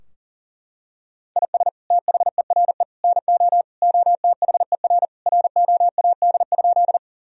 Morse code, heard as a keyed tone, about 40 wpm, 700 Hz. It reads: IS THERE NO OTHER ROAD?
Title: Sentences from Top 1000 Words - Review Part 2 - 40wpm